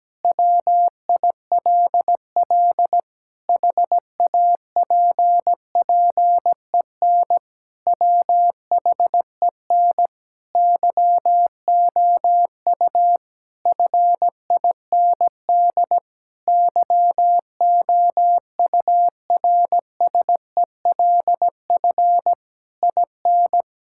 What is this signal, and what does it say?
Morse code, heard as a keyed tone, about 17 wpm, 700 Hz. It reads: WILL HAPPEN WHEN YOU FIND YOURSELF IN
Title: Sentences from Top 1000 Words - Review Part 5 - 17wpm